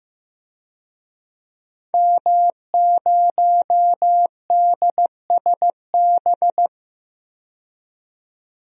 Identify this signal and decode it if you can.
Morse code, heard as a keyed tone, about 15 wpm, 700 Hz. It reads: M0DSB
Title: DXCC-100 Call Signs - 5 Characters - 15wpm